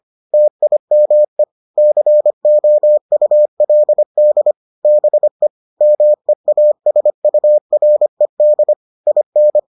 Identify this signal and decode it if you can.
Morse code, heard as a keyed tone, about 25 wpm, 600 Hz. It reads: TIME COULD BE MEASURED IN